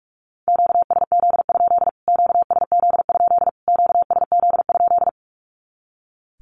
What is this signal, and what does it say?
Morse code, heard as a keyed tone, about 44 wpm, 700 Hz. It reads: QSZ? QSZ? QSZ?
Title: Q-Codes Repeated 3x 45wpm